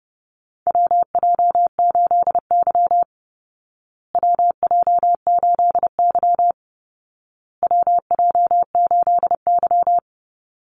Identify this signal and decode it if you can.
Morse code, heard as a keyed tone, about 30 wpm, 700 Hz. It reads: WJ8Y WJ8Y WJ8Y
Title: Call Signs Repeated 2x - 30wpm